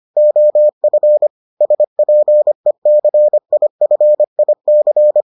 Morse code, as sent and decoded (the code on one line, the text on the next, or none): --- ..-. / ... .--. -.-. .. ..-. .. -.-.
OF SPECIFIC